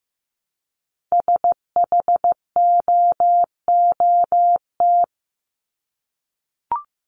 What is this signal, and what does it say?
Morse code, heard as a keyed tone, about 15 wpm, 700 Hz. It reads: SHOOT
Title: Top 1001-1250 Words 15wpm